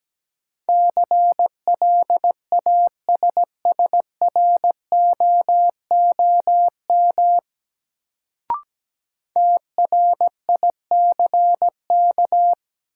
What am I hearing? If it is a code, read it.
Morse code, heard as a keyed tone, about 17 wpm, 700 Hz. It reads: CLASSROOM TRICK
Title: Sentences from Top 1500 Words - New Word Drill - 17wpm